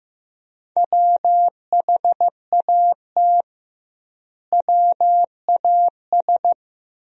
Morse code, heard as a keyed tone, about 16 wpm, 700 Hz. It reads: WHAT WAS
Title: 2x Word-Spacing - Sentences from Top 100 Words - 15wpm